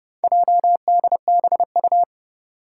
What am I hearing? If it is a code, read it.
Morse code, heard as a keyed tone, about 29 wpm, 700 Hz. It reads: JDBU
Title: Four Challenging Letter-Number Combinations 30wpm